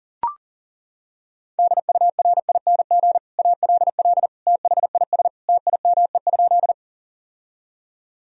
Morse code, heard as a keyed tone, about 40 wpm, 700 Hz. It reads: DURING ALL THIS TIME?
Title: Sentences from Top 1000 Words - Review Part 2 - 40wpm